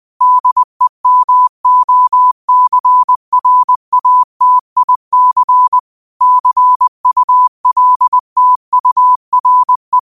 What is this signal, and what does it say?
Morse code, heard as a keyed tone, about 20 wpm, 1000 Hz. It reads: DEMOCRATIC CULTURE